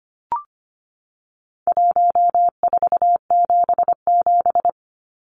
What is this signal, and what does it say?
Morse code, heard as a keyed tone, about 25 wpm, 700 Hz. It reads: 1477
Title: Numbers - Contest Style - 4-Digit 25wpm